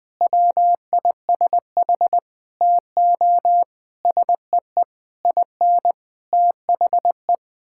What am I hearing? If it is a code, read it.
Morse code, heard as a keyed tone, about 21 wpm, 700 Hz. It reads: WISH TO SEE IN THE
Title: Sets of 9 Words - 20wpm